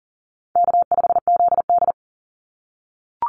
Morse code, heard as a keyed tone, about 40 wpm, 700 Hz. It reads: K5ZD
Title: Easy Call Signs - 40wpm